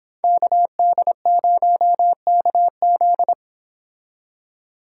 Morse code, heard as a keyed tone, about 31 wpm, 700 Hz. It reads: KD0KZ